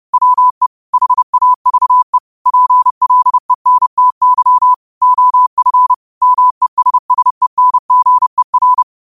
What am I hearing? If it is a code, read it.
Morse code, heard as a keyed tone, about 30 wpm, 1000 Hz. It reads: WE HAVE PLENTY OF MESSENGER